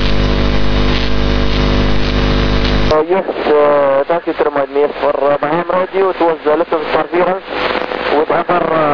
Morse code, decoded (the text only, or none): none